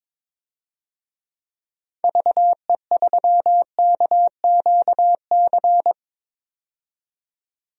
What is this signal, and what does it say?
Morse code, heard as a keyed tone, about 21 wpm, 700 Hz. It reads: VE3KQC
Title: DXCC-100 Call Signs - 6 Characters - 22wpm